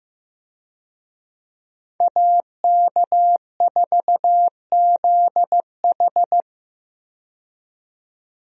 Morse code, heard as a keyed tone, about 15 wpm, 700 Hz. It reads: AK4ZH
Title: USA Call Signs - 5 Characters - 15wpm